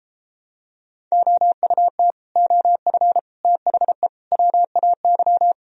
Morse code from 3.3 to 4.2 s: THE

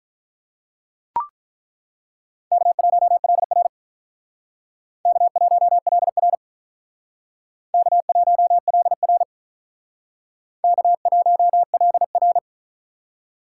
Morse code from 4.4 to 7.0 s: K1LR

